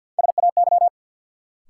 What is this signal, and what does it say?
Morse code, heard as a keyed tone, about 53 wpm, 700 Hz. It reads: SAY
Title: Sentences from Top 200 Words - New Word Drill - 50wpm